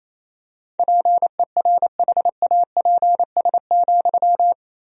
Morse code, heard as a keyed tone, about 28 wpm, 700 Hz. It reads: PERHAPS,
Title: 2x Word-Spacing - Sentences from Top 500 Words - 28wpm